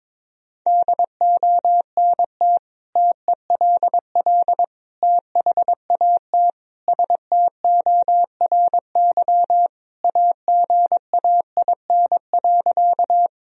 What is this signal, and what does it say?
Morse code, heard as a keyed tone, about 23 wpm, 700 Hz. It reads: DONT TELL THAT STORY AGAIN.
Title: Sentences from Top 1000 Words - Review Part 1 - 22wpm